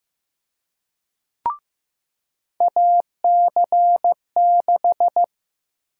Morse code, heard as a keyed tone, about 15 wpm, 700 Hz. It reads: AC6
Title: USA Call Sign Prefixes - Repeated 2x - 15wpm